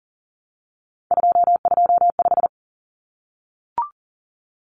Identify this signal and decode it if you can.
Morse code, heard as a keyed tone, about 40 wpm, 700 Hz. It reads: 225
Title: Common Cut Numbers - 3-Digit 40wpm